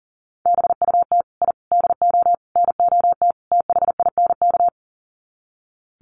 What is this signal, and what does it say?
Morse code, heard as a keyed tone, about 40 wpm, 700 Hz. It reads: BUT I DO NOT THINK
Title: Sets of 5 Words 40wpm